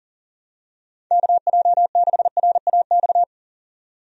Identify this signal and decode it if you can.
Morse code, heard as a keyed tone, about 38 wpm, 700 Hz. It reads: KJ6RAX
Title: USA Call Signs - 6 Characters - 40wpm